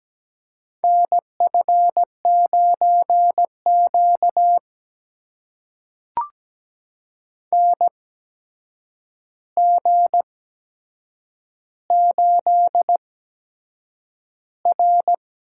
Morse code, heard as a keyed tone, about 17 wpm, spaced as for 7 wpm, 700 Hz. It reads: NF9Q NG8R
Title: USA Call Signs - 4 Characters - 8x Character Spacing - 17wpm